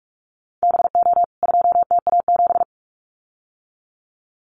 Morse code, heard as a keyed tone, about 44 wpm, 700 Hz. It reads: BO 2TA7
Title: CWT Exchange 45wpm